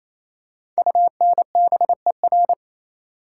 Morse code, heard as a keyed tone, about 28 wpm, 700 Hz. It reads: UNBER
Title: Code Groups - 5 Letters - 28wpm